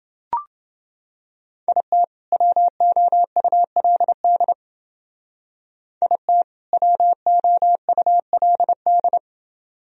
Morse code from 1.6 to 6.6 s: IT WOULD IT

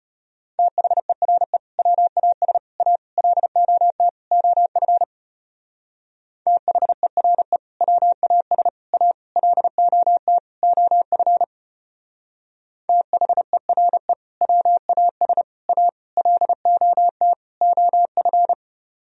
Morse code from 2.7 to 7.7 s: A LOT OF THERE

